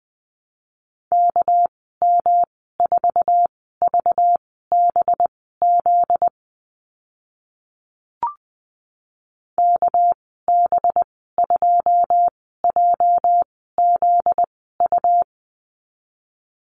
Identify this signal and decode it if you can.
Morse code, heard as a keyed tone, about 20 wpm, spaced as for 14 wpm, 700 Hz. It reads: KM4VBZ KB2JZU
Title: 245 Lesson 39g - Letter z - Call Signs 20wpm